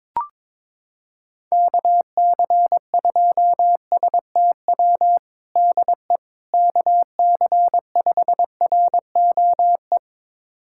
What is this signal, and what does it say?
Morse code, heard as a keyed tone, about 22 wpm, 700 Hz. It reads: KC2STW DE KC5ROE
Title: USA Callsign DE Callsign - 6 Characters - 22wpm